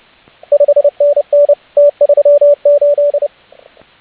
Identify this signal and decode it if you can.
Morse code, heard as a keyed tone, about 29 wpm, 580 Hz, with radio noise. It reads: 5NN T38